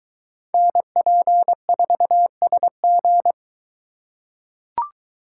Morse code, heard as a keyed tone, about 22 wpm, 700 Hz. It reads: NP4SG